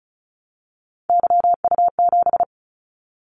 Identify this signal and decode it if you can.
Morse code, heard as a keyed tone, about 36 wpm, 700 Hz. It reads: YU7
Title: Callsign Prefixes for Top 25 DXCC - 35wpm